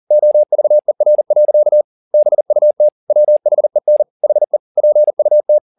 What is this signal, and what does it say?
Morse code, heard as a keyed tone, about 40 wpm, 600 Hz. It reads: OVER. BUT WHEN HE PUT
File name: the old man and the sea 2 40wpm